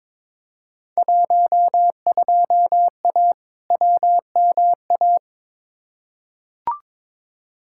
Morse code, heard as a keyed tone, about 22 wpm, 700 Hz. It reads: 12A WMA
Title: Field Day Exchanges Encore - 22wpm